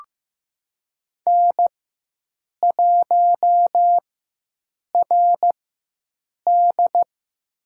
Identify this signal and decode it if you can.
Morse code, heard as a keyed tone, about 15 wpm, spaced as for 7 wpm, 700 Hz. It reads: N1RD
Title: USA Call Signs - 4 Characters - 4x Character Spacing - 15wpm